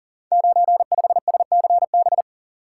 Morse code, heard as a keyed tone, about 42 wpm, 700 Hz. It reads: 95SCB
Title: Code Groups - 5 Letter:Numbers - 40wpm